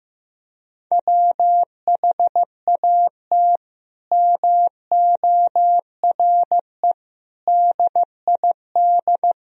Morse code, heard as a keyed tone, about 16 wpm, 700 Hz. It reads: WHAT MORE DID
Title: Sentences from Top 1000 Words - Review Part 3 - 15wpm